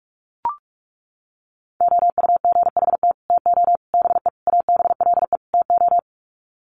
Morse code, heard as a keyed tone, about 44 wpm, 700 Hz. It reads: OUGHT TO BE ABLE TO
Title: Sets of 5 Words Encore2 45wpm